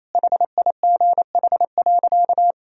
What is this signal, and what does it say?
Morse code, heard as a keyed tone, about 28 wpm, 700 Hz. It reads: HIGH.
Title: Sentences from Top 1000 Words - Review Part 1 - 28wpm